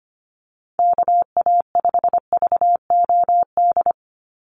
Morse code, heard as a keyed tone, about 25 wpm, 690 Hz. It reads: KA5VOD
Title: USA Call Signs 25wpm